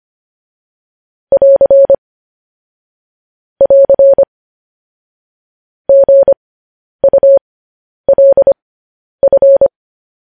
Morse code, heard as a keyed tone, about 25 wpm, spaced as for 10 wpm, 550 Hz. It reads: <AR> <AR> GULF
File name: podcast / News headlines-1x-25x10